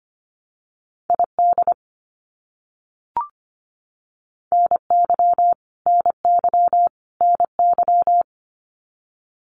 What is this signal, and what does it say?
Morse code, heard as a keyed tone, about 25 wpm, 700 Hz. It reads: ID NY NY NY
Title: US States Repeated 3x 25wpm